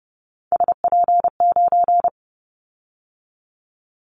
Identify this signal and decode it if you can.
Morse code, heard as a keyed tone, about 30 wpm, 700 Hz. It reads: SP9
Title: Callsign Prefixes for Top 25 DXCC - 30wpm